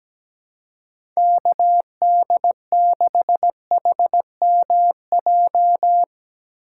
Morse code, heard as a keyed tone, about 17 wpm, 700 Hz. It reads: KD6HMJ
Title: USA Call Signs - 6 Characters - 2x Character Spacing - 17wpm